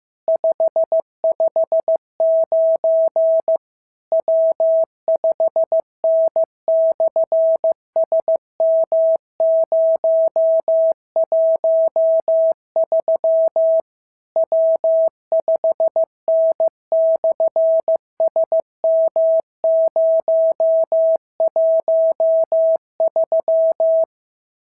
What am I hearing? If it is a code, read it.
Morse code, heard as a keyed tone, about 15 wpm, 650 Hz. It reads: 559 W5N/SM013 W5N/SM013
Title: SOTA QSO Practice 15wpm